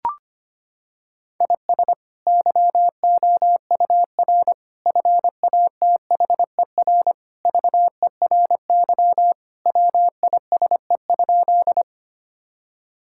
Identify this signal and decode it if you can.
Morse code, heard as a keyed tone, about 25 wpm, 700 Hz. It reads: IS YOUR FATHER VERY WISE?
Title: Sentences from Top 1600 Words 25wpm